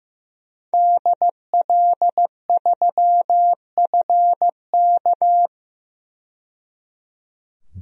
Morse code, heard as a keyed tone, about 15 wpm, 700 Hz. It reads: DL3FK
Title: DXCC-100 Call Signs - 5 Characters - 15wpm